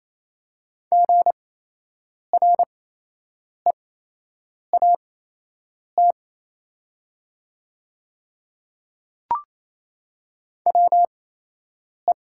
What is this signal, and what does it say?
Morse code, heard as a keyed tone, about 27 wpm, spaced as for 8 wpm, 700 Hz. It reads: GREAT WE
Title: 8x Character Spacing - Top 200 Words - 28wpm